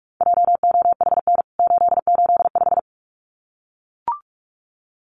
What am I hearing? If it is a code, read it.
Morse code, heard as a keyed tone, about 44 wpm, 700 Hz. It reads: JOHN 885
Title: CWT Exchange 45wpm